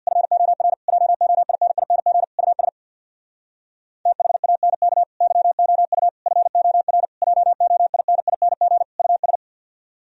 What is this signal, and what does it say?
Morse code, heard as a keyed tone, about 64 wpm, 700 Hz. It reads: FOR JOINING US THANK YOU FOR JOINING US